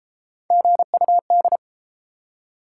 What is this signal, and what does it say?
Morse code, heard as a keyed tone, about 31 wpm, 700 Hz. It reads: GUD